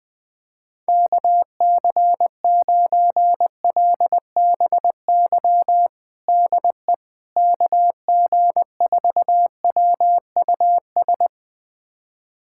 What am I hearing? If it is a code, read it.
Morse code, heard as a keyed tone, about 21 wpm, 700 Hz. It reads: KC9LBY DE KG4WUS
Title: USA Callsign DE Callsign - 6 Characters - 20wpm